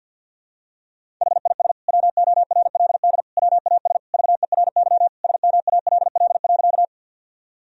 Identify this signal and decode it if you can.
Morse code, heard as a keyed tone, about 52 wpm, 700 Hz. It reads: HIS WORLD WAS VERY SMALL.
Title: Sentences from Top 1000 Words - Review Part 1 - 50wpm